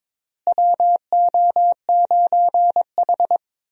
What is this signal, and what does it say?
Morse code, heard as a keyed tone, about 21 wpm, 700 Hz. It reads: WO9H
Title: USA Call Signs - 4 Characters - Repeated 3x - 22wpm